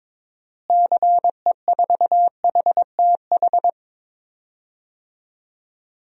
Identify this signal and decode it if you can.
Morse code, heard as a keyed tone, about 22 wpm, 700 Hz. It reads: CE4HTH
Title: DXCC-100 Call Signs - 6 Characters - Repeated 2x - 22wpm